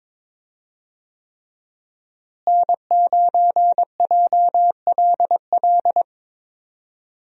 Morse code, heard as a keyed tone, about 22 wpm, 700 Hz. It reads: N9JLL
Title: USA Call Signs - 5 Characters - Repeated 3x - 22wpm